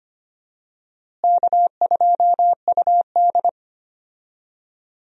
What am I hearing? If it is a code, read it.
Morse code, heard as a keyed tone, about 25 wpm, 700 Hz. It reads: K2UD
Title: USA Call Signs - 4 Characters - Repeated 2x - 25wpm